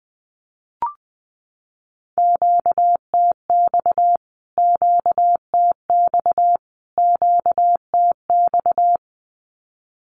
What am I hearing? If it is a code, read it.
Morse code, heard as a keyed tone, about 20 wpm, 700 Hz. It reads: QTX QTX QTX
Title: Q-Codes Repeated 3x 20wpm